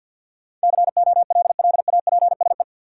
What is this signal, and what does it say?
Morse code, heard as a keyed tone, about 50 wpm, 700 Hz. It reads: COLLAPSE
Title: Sets of 5 Words Encore - New Word Drill - 50wpm